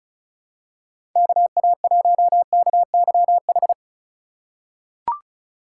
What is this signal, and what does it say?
Morse code, heard as a keyed tone, about 36 wpm, 700 Hz. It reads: KA1KYH
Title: USA Call Signs - 6 Characters - 4x Character Spacing - 35wpm